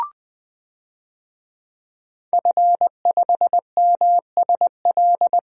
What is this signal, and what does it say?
Morse code, heard as a keyed tone, about 20 wpm, 700 Hz. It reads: F5MSL
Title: DXCC-100 Call Signs - 5 Characters - 20wpm